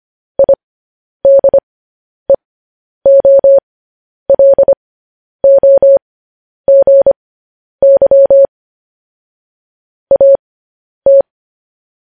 Morse code, heard as a keyed tone, about 25 wpm, spaced as for 10 wpm, 550 Hz. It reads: IDEOLOGY AT
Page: news-headlines-in-morse-code-at-10-wpm